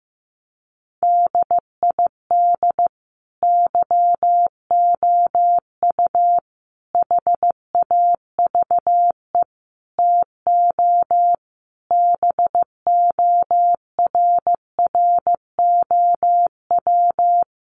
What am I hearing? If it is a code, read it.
Morse code, heard as a keyed tone, about 15 wpm, 700 Hz. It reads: DID YOU HAVE TO BORROW
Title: Sentences from Top 1500 Words 15wpm